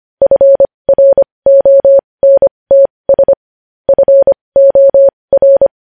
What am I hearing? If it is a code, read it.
Morse code, heard as a keyed tone, about 25 wpm, spaced as for 20 wpm, 550 Hz. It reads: FRONTS FOR
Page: News Headlines in Morse Code at 20 WPM